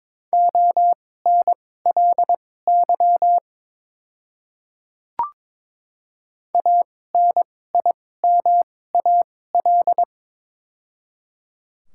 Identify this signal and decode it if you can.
Morse code, heard as a keyed tone, about 22 wpm, spaced as for 16 wpm, 700 Hz. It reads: ONLY ANIMAL
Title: 2x Character Spacing - Top 200 Words - 22wpm